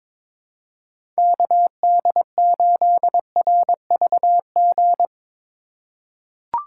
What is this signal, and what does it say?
Morse code, heard as a keyed tone, about 21 wpm, 700 Hz. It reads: KD8RVG
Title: USA Call Signs - 6 Characters - 3x Character Spacing - 22wpm